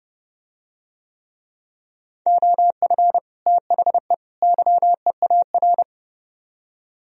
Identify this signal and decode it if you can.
Morse code, heard as a keyed tone, about 28 wpm, 700 Hz. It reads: OF THE YEAR